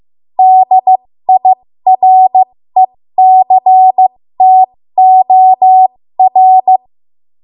Morse code, heard as a keyed tone, about 15 wpm, spaced as for 13 wpm, 750 Hz. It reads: DIRECTOR
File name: Morse / 13WPM